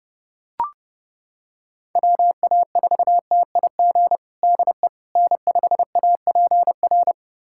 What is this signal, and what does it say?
Morse code, heard as a keyed tone, about 30 wpm, 700 Hz. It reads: WA4TIG DE N5APR
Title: USA Callsign DE Callsign 30wpm